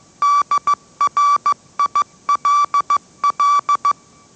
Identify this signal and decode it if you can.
Morse code, heard as a keyed tone, about 15 wpm, 1160 Hz. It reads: DRILL
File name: mrx / mrx065 16wpm